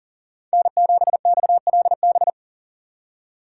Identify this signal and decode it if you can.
Morse code, heard as a keyed tone, about 40 wpm, 700 Hz. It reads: N7XLB
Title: USA Call Signs - 4x Character Spacing - 40wpm